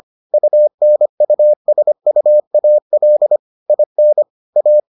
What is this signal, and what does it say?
Morse code, heard as a keyed tone, about 25 wpm, 600 Hz. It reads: UNUSUAL IN A